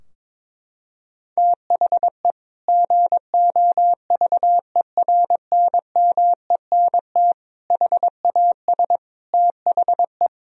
Morse code, heard as a keyed tone, about 22 wpm, 700 Hz. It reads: THE GOVERNMENT HAS THE
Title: Sentences from Top 1000 Words - Review Part 3 - 22wpm